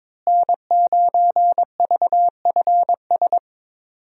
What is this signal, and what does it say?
Morse code, heard as a keyed tone, about 21 wpm, 700 Hz. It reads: N9VFS
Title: USA Call Signs - 5 Characters - Repeated 2x - 22wpm